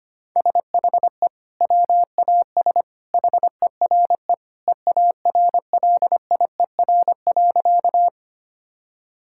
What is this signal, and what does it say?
Morse code, heard as a keyed tone, about 25 wpm, 700 Hz. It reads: SHE WAS HERE EARLIER.
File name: practice / 25wpm / Sentences with 2-Word Phrases - 25wpm